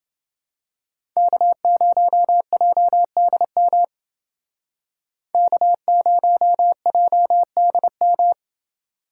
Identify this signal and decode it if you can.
Morse code, heard as a keyed tone, about 28 wpm, 700 Hz. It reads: K0JDM K0JDM